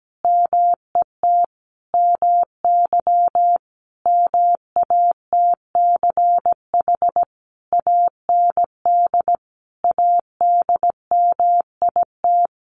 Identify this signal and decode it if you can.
Morse code, heard as a keyed tone, about 17 wpm, 700 Hz. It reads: MET MY MATCH AND ADMIT
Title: A Sound Of Thunder 17wpm